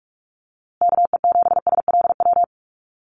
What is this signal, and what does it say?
Morse code, heard as a keyed tone, about 44 wpm, 700 Hz. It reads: KE7SLW
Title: USA Call Signs Repeated 2x 45wpm